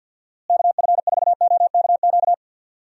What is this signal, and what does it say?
Morse code, heard as a keyed tone, about 46 wpm, 700 Hz. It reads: KF4OKQ
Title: USA Call Signs - 6 Characters - Repeated 3x - 50wpm